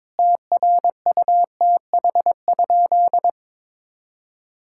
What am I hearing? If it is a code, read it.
Morse code, heard as a keyed tone, about 22 wpm, 700 Hz. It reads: TRUTH?
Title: Sentences from Top 1600 Words 22wpm